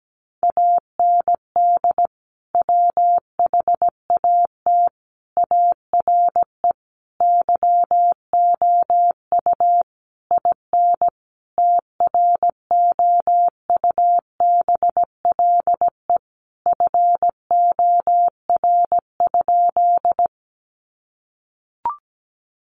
Morse code, heard as a keyed tone, about 17 wpm, 700 Hz. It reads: AND WHAT ARE YOU IN TROUBLE FOR?
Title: Sentences from Top 1200 Words 17wpm